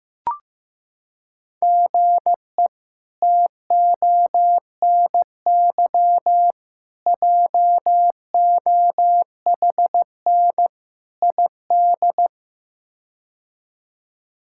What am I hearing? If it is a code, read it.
Morse code, heard as a keyed tone, about 16 wpm, 700 Hz. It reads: GE TONY JOHN ID
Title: K1USN SST QSO Practice - 15wpm